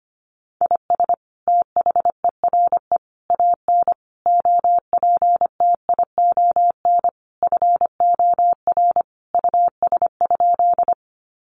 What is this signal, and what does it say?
Morse code, heard as a keyed tone, about 25 wpm, 700 Hz. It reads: IS THERE AN OPTION FOR US?
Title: Sentences from Top 1300 Words 25wpm